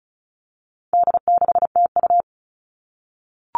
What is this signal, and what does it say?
Morse code, heard as a keyed tone, about 34 wpm, 700 Hz. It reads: D6TU
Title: All Cut Numbers - 4-Digit 35wpm